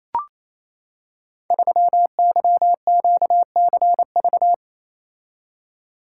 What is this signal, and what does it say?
Morse code, heard as a keyed tone, about 27 wpm, 700 Hz. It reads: 3YQCV
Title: Code Groups - 5 Letter:Numbers - 28wpm